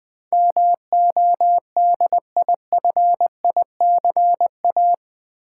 Morse code, heard as a keyed tone, about 20 wpm, 700 Hz. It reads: MODIFICA
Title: Build A Word - 4 Letters At A Time - 12-Letter Words - 20wpm